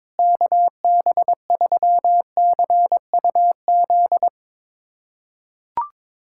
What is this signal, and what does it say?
Morse code, heard as a keyed tone, about 22 wpm, 700 Hz. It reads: KB3CUZ
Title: USA Callsign DE Callsign 22wpm